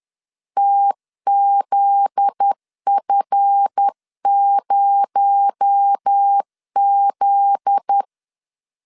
Morse code, heard as a keyed tone, about 10 wpm, 800 Hz. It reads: TZF0Z
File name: LEZIONI / 7 GF9Z / 7 GF9Z 10wpm 2611